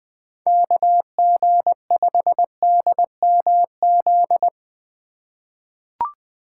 Morse code, heard as a keyed tone, about 20 wpm, 700 Hz. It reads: KG5DMZ
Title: USA Call Signs - 6 Characters - 2x Character Spacing - 20wpm